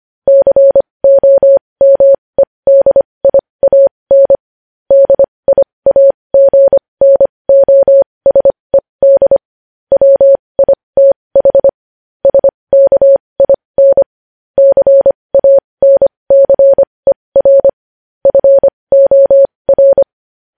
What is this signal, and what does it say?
Morse code, heard as a keyed tone, about 25 wpm, spaced as for 20 wpm, 550 Hz. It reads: COMEDIAN DIAGNOSED WITH SKIN CANCER FOR
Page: news-headlines-in-morse-code-at-20-wpm